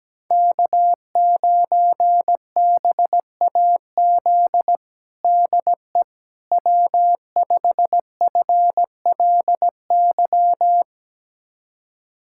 Morse code, heard as a keyed tone, about 17 wpm, 700 Hz. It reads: K9BAZ DE W5FLY
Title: USA Callsign DE Callsign - 5 Characters - 17wpm